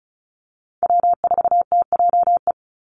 Morse code, heard as a keyed tone, about 34 wpm, 700 Hz. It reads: W4TJE
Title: Call Signs Repeated 2x - 35wpm